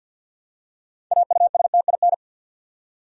Morse code, heard as a keyed tone, about 53 wpm, 700 Hz. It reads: AUSTIN
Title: USA Common Names - 50wpm